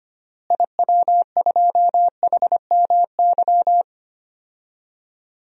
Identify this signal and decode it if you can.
Morse code, heard as a keyed tone, about 25 wpm, 700 Hz. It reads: IW2HMY
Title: DXCC-100 Call Signs - 6 Characters - Repeated 2x - 25wpm